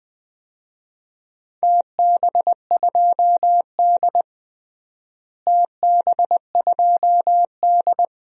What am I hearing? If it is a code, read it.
Morse code, heard as a keyed tone, about 20 wpm, 700 Hz. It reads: TB2D TB2D
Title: DXCC-100 Call Signs - 4 Characters - Repeated 2x - 20wpm